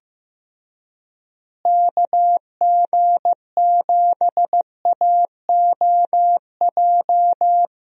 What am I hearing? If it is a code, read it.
Morse code, heard as a keyed tone, about 15 wpm, 700 Hz. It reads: KG7AOJ
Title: USA Call Signs - 6 Characters - Repeated 3x - 15wpm